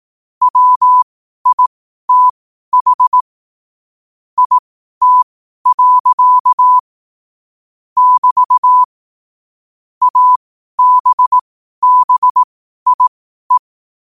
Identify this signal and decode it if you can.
Morse code, heard as a keyed tone, about 18 wpm, spaced as for 12 wpm, 1000 Hz. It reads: WITH IT. <BT> ABBIE